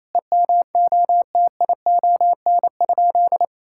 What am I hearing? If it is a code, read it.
Morse code, heard as a keyed tone, about 28 wpm, 700 Hz. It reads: EMOTION?
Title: Sentences from Top 1600 Words 28wpm